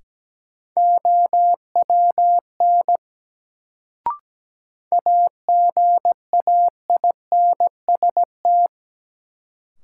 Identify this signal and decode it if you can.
Morse code, heard as a keyed tone, about 17 wpm, 700 Hz. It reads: OWN AGAINST
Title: Sets of 2 Words - New Word Drill - 17wpm